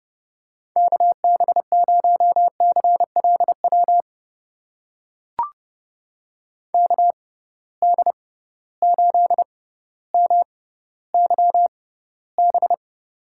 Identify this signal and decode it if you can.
Morse code, heard as a keyed tone, about 30 wpm, spaced as for 15 wpm, 700 Hz. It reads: KB0CLW KD8MYB
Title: USA Call Signs - 6 Characters - 6x Character Spacing - 30wpm